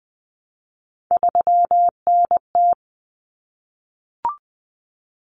Morse code, heard as a keyed tone, about 20 wpm, 700 Hz. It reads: 3NT
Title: Common Cut Numbers - 3-Digit 20wpm